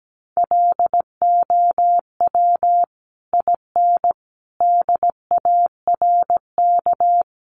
Morse code, heard as a keyed tone, about 17 wpm, 700 Hz. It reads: LOW IN DARK
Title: Sentences from Top 900 Words 17wpm